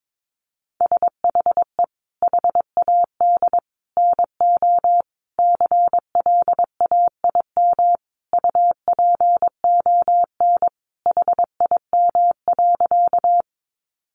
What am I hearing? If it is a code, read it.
Morse code, heard as a keyed tone, about 22 wpm, 700 Hz. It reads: SHE HAD NO CLAIM UPON HIM.
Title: Sentences from Top 1500 Words 22wpm